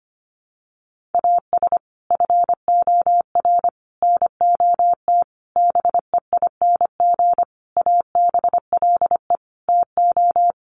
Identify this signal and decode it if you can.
Morse code, heard as a keyed tone, about 25 wpm, 700 Hz. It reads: AS FOR NOT BEING ABLE TO
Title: Sentences from Top 1100 Words 25wpm